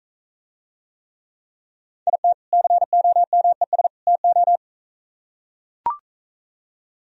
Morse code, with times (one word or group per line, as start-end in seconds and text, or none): 2.1-2.3 IT
2.5-3.9 COMES
4.1-4.6 TO